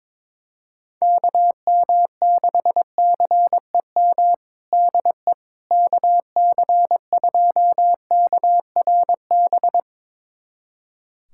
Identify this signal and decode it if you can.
Morse code, heard as a keyed tone, about 22 wpm, 700 Hz. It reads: KM6CEM DE KC2KRB
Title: USA Callsign DE Callsign - 6 Characters - 22wpm